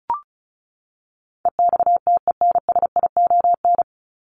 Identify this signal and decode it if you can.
Morse code, heard as a keyed tone, about 34 wpm, 700 Hz. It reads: EXTENSION
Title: Top 1501-1750 Words 35wpm